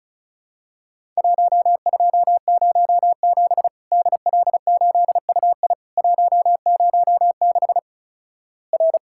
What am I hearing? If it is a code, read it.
Morse code, heard as a keyed tone, about 36 wpm, 700 Hz. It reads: 1207 DL8UI 106 R
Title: Worked All Europe DX Contest - QTC Exchange - 35wpm